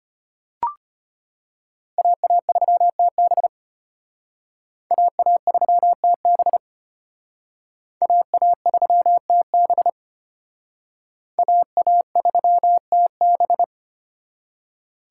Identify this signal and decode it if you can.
Morse code, heard as a keyed tone, about 31 wpm, 700 Hz. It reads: AA3TB AA3TB AA3TB AA3TB